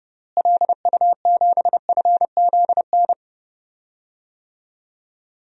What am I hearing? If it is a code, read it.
Morse code, heard as a keyed tone, about 30 wpm, 700 Hz. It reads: LU7FZN
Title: DXCC-100 Call Signs - 6 Characters - Repeated 2x - 30wpm